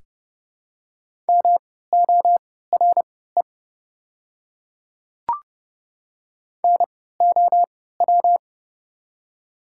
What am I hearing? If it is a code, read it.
Morse code, heard as a keyed tone, about 30 wpm, spaced as for 17 wpm, 690 Hz. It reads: MORE NOW
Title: 3x Character Spacing - Top 500 Words - 30wpm